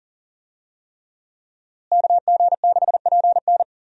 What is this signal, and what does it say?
Morse code, heard as a keyed tone, about 40 wpm, 700 Hz. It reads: KG6PN
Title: USA Call Signs - 5 Characters - Repeated 3x - 40wpm